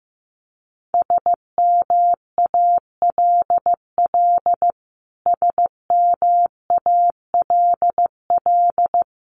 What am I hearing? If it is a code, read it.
Morse code, heard as a keyed tone, about 15 wpm, 700 Hz. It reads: SMALL SMALL
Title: Top 500 Words Repeated 15wpm